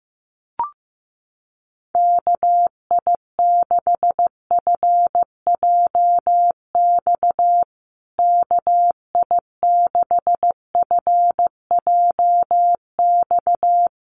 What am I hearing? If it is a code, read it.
Morse code, heard as a keyed tone, about 15 wpm, 700 Hz. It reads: KI6FJX KI6FJX
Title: USA Call Signs Repeated 3x 15wpm